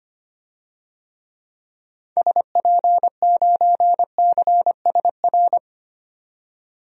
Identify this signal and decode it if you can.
Morse code, heard as a keyed tone, about 25 wpm, 700 Hz. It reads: SP9CSR
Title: DXCC-100 Call Signs - 6 Characters - 25wpm